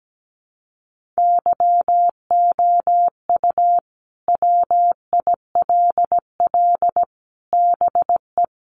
Morse code, heard as a keyed tone, about 17 wpm, 700 Hz. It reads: YOU WILL BE
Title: Sentences from Top 1000 Words 17wpm